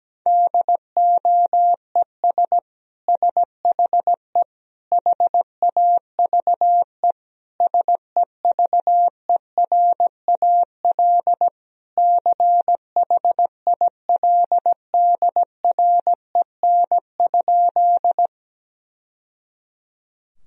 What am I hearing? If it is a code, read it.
Morse code, heard as a keyed tone, about 17 wpm, 700 Hz. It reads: DOES SHE HAVE SEVERAL CHILDREN?
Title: Sentences from Top 1000 Words - Review Part 2 - 17wpm